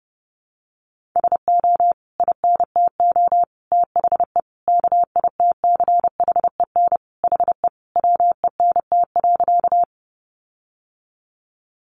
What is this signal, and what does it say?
Morse code, heard as a keyed tone, about 30 wpm, 700 Hz. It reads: SO INTO THE KITCHEN HE WENT.
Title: Sentences from Top 1400 Words 30wpm